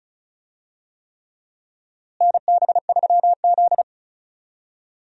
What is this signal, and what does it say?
Morse code, heard as a keyed tone, about 34 wpm, 700 Hz. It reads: NB3Z
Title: USA Call Signs - 4 Characters - Repeated 3x - 35wpm